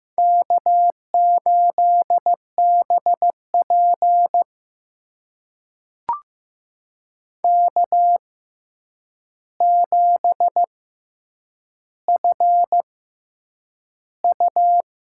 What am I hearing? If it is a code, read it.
Morse code, heard as a keyed tone, about 15 wpm, spaced as for 8 wpm, 700 Hz. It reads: K8BP K7FU
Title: USA Call Signs - 4 Characters - 6x Character Spacing - 15wpm